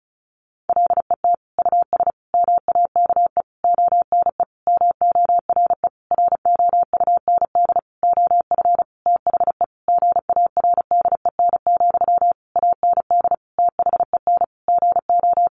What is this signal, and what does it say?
Morse code, heard as a keyed tone, about 34 wpm, 700 Hz. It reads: LET US MAKE ONE MORE ROUND OF THE GARDEN, AND THEN GO
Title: Sentences from Top 700 Words 35wpm